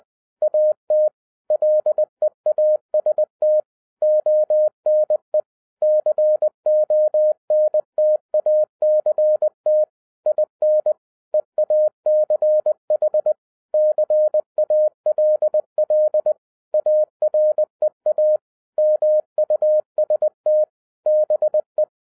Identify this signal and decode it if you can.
Morse code, heard as a keyed tone, about 20 wpm, 610 Hz. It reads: AT LEAST ONE CONTACT IN EACH CALL AREA MUST BE